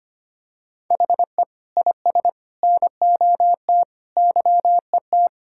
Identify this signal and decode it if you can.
Morse code, heard as a keyed tone, about 25 wpm, 700 Hz. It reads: HE IS NOT YET
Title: Sentences from Top 1000 Words - Review Part 6 - 25wpm